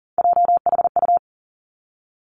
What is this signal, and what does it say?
Morse code, heard as a keyed tone, about 42 wpm, 700 Hz. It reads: JHU
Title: Three Challenging Letter-Number Combinations 40wpm